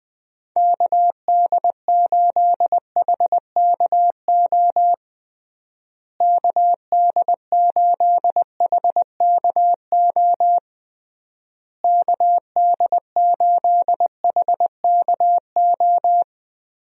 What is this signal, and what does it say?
Morse code, heard as a keyed tone, about 20 wpm, 700 Hz. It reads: KD8HKO KD8HKO KD8HKO
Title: USA Call Signs - 6 Characters - Repeated 3x - 20wpm